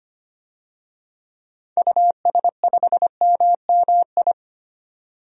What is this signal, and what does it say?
Morse code, heard as a keyed tone, about 25 wpm, 700 Hz. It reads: US5MMI
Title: DXCC-100 Call Signs - 6 Characters - 25wpm